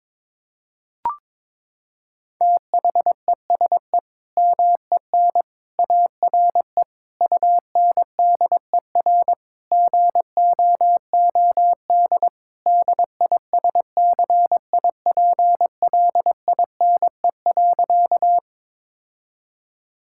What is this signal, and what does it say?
Morse code, heard as a keyed tone, about 22 wpm, 700 Hz. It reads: THESE MEN ARE UNDER GOOD DISCIPLINE.
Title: Sentences from Top 1000 Words - Review Part 6 - 22wpm